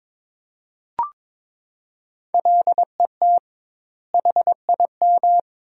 Morse code, heard as a keyed tone, about 22 wpm, 700 Hz. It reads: LET HIM
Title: 2x Word-Spacing - Sentences from Top 300 Words - 22wpm